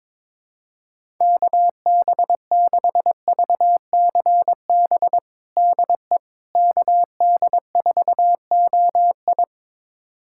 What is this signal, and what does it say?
Morse code, heard as a keyed tone, about 22 wpm, 700 Hz. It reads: KB6VCB DE KD4OI
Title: USA Callsign DE Callsign 22wpm